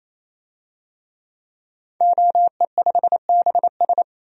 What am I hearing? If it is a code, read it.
Morse code, heard as a keyed tone, about 28 wpm, 700 Hz. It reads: OE5BS
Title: DXCC-100 Call Signs - 5 Characters - Repeated 3x - 28wpm